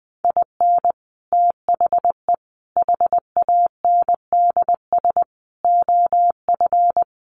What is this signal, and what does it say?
Morse code, heard as a keyed tone, about 20 wpm, 700 Hz. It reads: IN THE HANDS OF
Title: Sets of 5 Words 20wpm